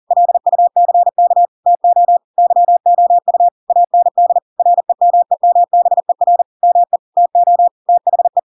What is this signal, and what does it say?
Morse code, heard as a keyed tone, about 40 wpm, 700 Hz. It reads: LUCK TO YOU AND REMEMBER ME TO THE
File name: Pinocchio / Pinocchio ch5 40 40 ik1hge